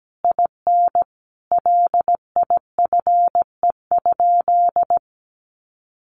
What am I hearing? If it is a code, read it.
Morse code, heard as a keyed tone, about 17 wpm, 700 Hz. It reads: IN LIFE?
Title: Sentences from Top 1400 Words 17wpm